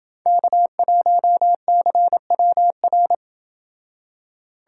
Morse code, heard as a keyed tone, about 27 wpm, 700 Hz. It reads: K1CWR